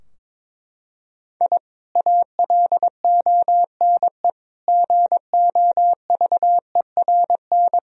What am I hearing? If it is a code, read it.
Morse code, heard as a keyed tone, about 22 wpm, 700 Hz. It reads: I ALONE GOVERN
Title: Sentences from Top 1000 Words - Review Part 2 - 22wpm